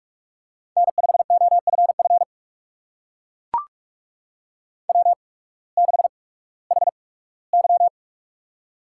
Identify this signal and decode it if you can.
Morse code, heard as a keyed tone, about 43 wpm, spaced as for 21 wpm, 700 Hz. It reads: N5OFF W6HY
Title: USA Call Signs - 8x Character Spacing - 45wpm